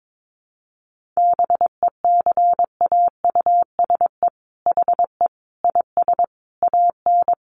Morse code, heard as a keyed tone, about 22 wpm, 700 Hz. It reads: BECAUSE HE IS AN
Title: Sentences from Top 1500 Words 22wpm